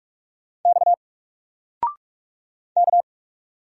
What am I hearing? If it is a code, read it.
Morse code, heard as a keyed tone, about 47 wpm, 700 Hz. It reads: X K
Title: Single Letter-Number 45wpm